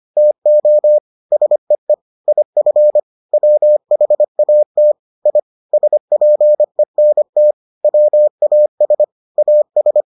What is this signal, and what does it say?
Morse code, heard as a keyed tone, about 25 wpm, 600 Hz. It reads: TO SEE IF WHAT I SPENT WAS AS